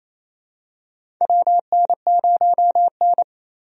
Morse code, heard as a keyed tone, about 28 wpm, 700 Hz. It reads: WN0N
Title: USA Call Signs - 4 Characters - Repeated 2x - 28wpm